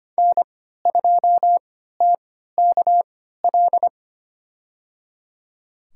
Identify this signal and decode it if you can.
Morse code, heard as a keyed tone, about 25 wpm, spaced as for 14 wpm, 700 Hz. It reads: N2TKL
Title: USA Call Signs - 5 Characters - 3x Character Spacing - 25wpm